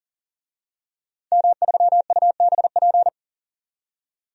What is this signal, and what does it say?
Morse code, heard as a keyed tone, about 37 wpm, 700 Hz. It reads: M3UBP
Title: DXCC-100 Call Signs - 5 Characters - 40wpm